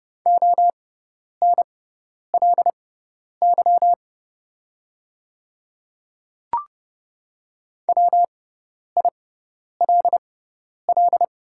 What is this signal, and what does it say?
Morse code, heard as a keyed tone, about 30 wpm, spaced as for 10 wpm, 690 Hz. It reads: ONLY WILL
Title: 6x Character Spacing - Top 200 Words - 30wpm